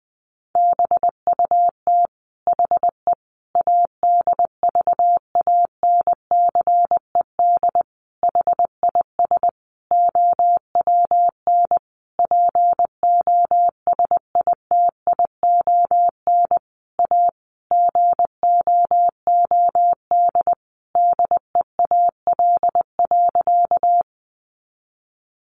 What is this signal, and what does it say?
Morse code, heard as a keyed tone, about 20 wpm, 700 Hz. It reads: BUT HE ADVANCED HIS OWN POSITION A GOOD DEAL.
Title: Sentences from Top 1200 Words 20wpm